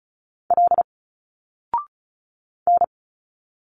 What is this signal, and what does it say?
Morse code, heard as a keyed tone, about 36 wpm, 700 Hz. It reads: L N
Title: Single Letters 35wpm